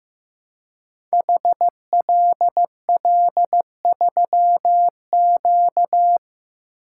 Morse code, heard as a keyed tone, about 15 wpm, 700 Hz. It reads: HLL3Q
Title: Code Groups - 5 Letter:Numbers - 15wpm